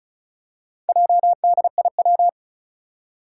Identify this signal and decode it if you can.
Morse code, heard as a keyed tone, about 36 wpm, 700 Hz. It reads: JDIW
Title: Four Challenging Letter-Number Combinations 35wpm